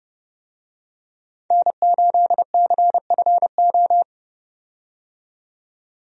Morse code, heard as a keyed tone, about 30 wpm, 700 Hz. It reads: N8CFO
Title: USA Call Signs - 5 Characters - 30wpm